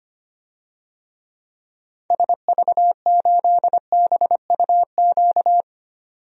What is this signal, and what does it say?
Morse code, heard as a keyed tone, about 25 wpm, 700 Hz. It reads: SV8BUQ
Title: DXCC-100 Call Signs - 6 Characters - 25wpm